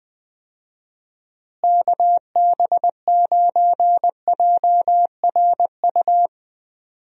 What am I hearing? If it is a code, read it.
Morse code, heard as a keyed tone, about 20 wpm, 700 Hz. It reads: KB9JRU
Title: USA Call Signs - 6 Characters - Repeated 2x - 20wpm